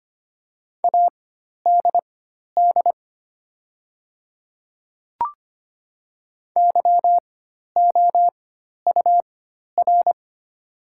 Morse code, heard as a keyed tone, about 25 wpm, spaced as for 12 wpm, 700 Hz. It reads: ADD YOUR
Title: 4x Character Spacing - Top 200 Words - 25wpm